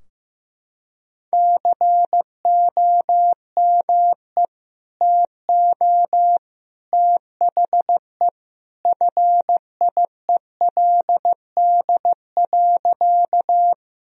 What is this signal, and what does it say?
Morse code, heard as a keyed tone, about 15 wpm, 700 Hz. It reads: COME TO THE FIELD.
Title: Sentences from Top 1000 Words - Review Part 2 - 15wpm